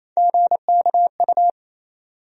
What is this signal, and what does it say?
Morse code, heard as a keyed tone, about 28 wpm, 700 Hz. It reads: GKU
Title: USA Call Sign Postfixes - 3 Letters 28wpm